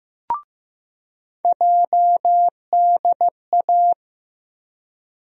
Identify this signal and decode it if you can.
Morse code, heard as a keyed tone, about 16 wpm, 700 Hz. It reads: JDA
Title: USA Call Sign Postfixes - 3 Letters 15wpm